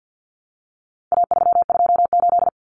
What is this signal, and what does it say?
Morse code, heard as a keyed tone, about 50 wpm, 700 Hz. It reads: A328
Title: Common Cut Numbers - 4-Digit 50wpm